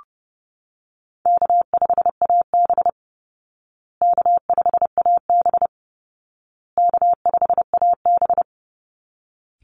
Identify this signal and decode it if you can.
Morse code, heard as a keyed tone, about 30 wpm, 700 Hz. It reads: K5AB K5AB K5AB
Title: Easy Call Signs Repeated 3x - 30wpm